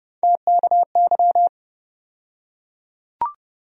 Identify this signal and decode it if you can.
Morse code, heard as a keyed tone, about 30 wpm, 700 Hz. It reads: TKY
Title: USA Call Sign Postfixes - 3 Letters 30wpm